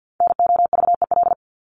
Morse code, heard as a keyed tone, about 50 wpm, 700 Hz. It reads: NOVEL
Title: Top 1001-1250 Words 50wpm